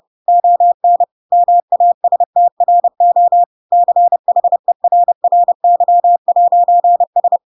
Morse code, heard as a keyed tone, about 30 wpm, 700 Hz. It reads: ON MASTRO CHERRY'S
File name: Pinocchio / Pinocchio ch1 30 30 ik1hge